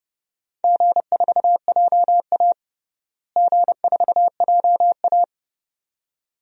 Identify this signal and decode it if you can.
Morse code, heard as a keyed tone, about 30 wpm, 700 Hz. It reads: G4JA G4JA
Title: DXCC-100 Call Signs - 4 Characters - Repeated 2x - 30wpm